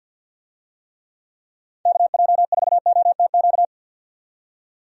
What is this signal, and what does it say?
Morse code, heard as a keyed tone, about 51 wpm, 700 Hz. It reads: KJ4OTQ
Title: USA Call Signs - 6 Characters - 50wpm